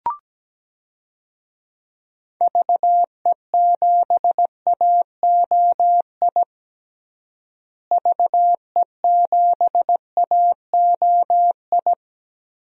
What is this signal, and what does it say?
Morse code, heard as a keyed tone, about 17 wpm, 700 Hz. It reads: VE7AOI VE7AOI
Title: DXCC-100 Call Signs - 6 Characters - Repeated 3x - 17wpm